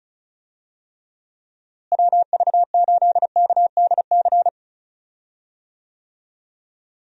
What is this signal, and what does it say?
Morse code, heard as a keyed tone, about 36 wpm, 700 Hz. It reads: WV8KDC
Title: USA Call Signs - 6 Characters - Repeated 2x - 35wpm